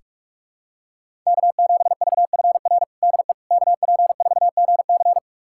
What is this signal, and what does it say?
Morse code, heard as a keyed tone, about 42 wpm, 700 Hz. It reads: K7VFR DE KP4GC
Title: USA Callsign DE Callsign - 5 Characters - 45wpm